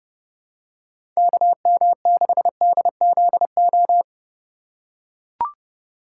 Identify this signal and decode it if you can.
Morse code, heard as a keyed tone, about 30 wpm, 700 Hz. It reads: KM6DZO
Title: USA Call Signs - 6 Characters - 2x Character Spacing - 30wpm